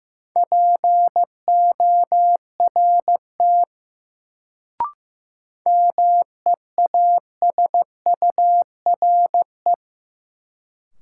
Morse code, heard as a keyed tone, about 15 wpm, 700 Hz. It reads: PORT MEASURE
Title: Sentences from Top 1000 Words - Review Part 1 - New Word Drill - 15wpm